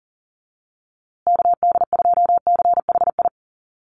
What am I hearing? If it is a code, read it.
Morse code, heard as a keyed tone, about 40 wpm, 700 Hz. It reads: KD2CHI
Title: USA Call Signs Repeated 2x 40wpm